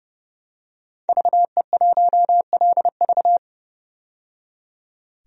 Morse code, heard as a keyed tone, about 30 wpm, 700 Hz. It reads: VE1LV
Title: DXCC-100 Call Signs - 5 Characters - 30wpm